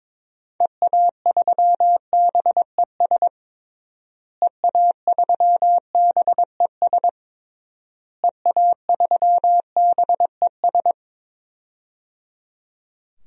Morse code, heard as a keyed tone, about 22 wpm, 700 Hz. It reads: EA3BES EA3BES EA3BES
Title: DXCC-100 Call Signs - 6 Characters - Repeated 3x - 22wpm